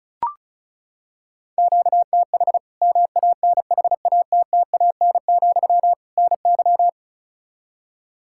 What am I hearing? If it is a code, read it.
Morse code, heard as a keyed tone, about 34 wpm, 700 Hz. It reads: QTH MANHATTAN, NY
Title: Practice QSO - Exchange 1 - Beginning - 35wpm